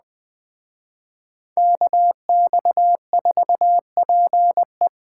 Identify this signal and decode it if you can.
Morse code, heard as a keyed tone, about 20 wpm, 700 Hz. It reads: KX4PE
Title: Speed Racing - USA Call Signs 17wpm